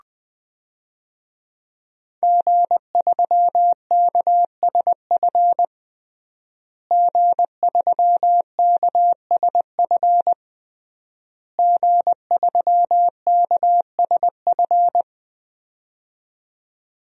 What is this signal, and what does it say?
Morse code, heard as a keyed tone, about 20 wpm, 700 Hz. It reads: G3KSF G3KSF G3KSF
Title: DXCC-100 Call Signs - 5 Characters - Repeated 3x - 20wpm